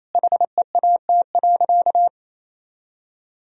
Morse code, heard as a keyed tone, about 28 wpm, 700 Hz. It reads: HEAT.
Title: Sentences from Top 1000 Words - Review Part 3 - 28wpm